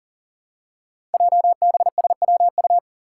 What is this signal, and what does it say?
Morse code, heard as a keyed tone, about 40 wpm, 700 Hz. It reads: JBSWU
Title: Code Groups - 5 Letter:Numbers - 40wpm